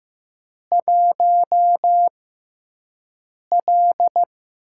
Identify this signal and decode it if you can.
Morse code, heard as a keyed tone, about 15 wpm, spaced as for 5 wpm, 700 Hz. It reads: 1L
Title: USA Call Signs - 6x Character Spacing - 15wpm